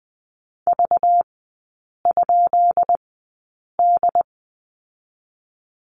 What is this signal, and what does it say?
Morse code, heard as a keyed tone, about 20 wpm, spaced as for 8 wpm, 690 Hz. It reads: V?D
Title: 212 Lesson 34e - Number 8 - Three Characters 20wpm